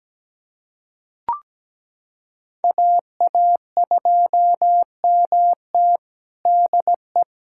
Morse code, heard as a keyed tone, about 17 wpm, 700 Hz. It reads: AA2MT DE
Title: USA Callsign DE Callsign 17wpm